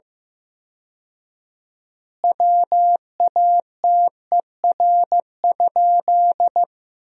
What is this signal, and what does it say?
Morse code, heard as a keyed tone, about 15 wpm, 700 Hz. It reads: WATER?
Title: 4x Word-Spacing - Sentences from Top 200 Words - 15wpm